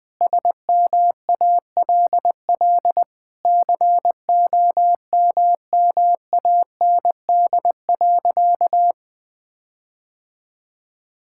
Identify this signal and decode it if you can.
Morse code, heard as a keyed tone, about 20 wpm, 700 Hz. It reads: SMALL COMMAND.
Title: Sentences from Top 1600 Words 20wpm